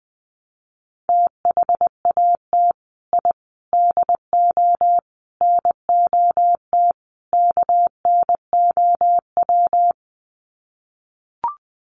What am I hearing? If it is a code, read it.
Morse code, heard as a keyed tone, about 20 wpm, 700 Hz. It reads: THAT I DO NOT KNOW
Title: Sets of 5 Words Encore2 20wpm